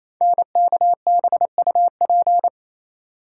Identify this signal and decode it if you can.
Morse code, heard as a keyed tone, about 29 wpm, 700 Hz. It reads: NKBUP
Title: Code Groups - 5 Letters - 28wpm